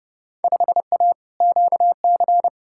Morse code, heard as a keyed tone, about 31 wpm, 700 Hz. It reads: 5A QC
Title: Field Day Exchanges Encore - 30wpm